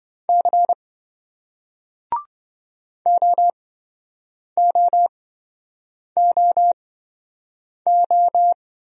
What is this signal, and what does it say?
Morse code, heard as a keyed tone, about 27 wpm, spaced as for 7 wpm, 700 Hz. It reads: C OOOO